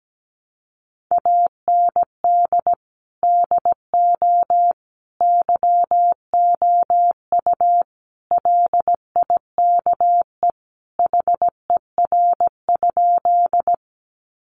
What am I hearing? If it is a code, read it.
Morse code, heard as a keyed tone, about 17 wpm, 700 Hz. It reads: AND DO YOU LIKE HER?
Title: Sentences from Top 100 Words 17wpm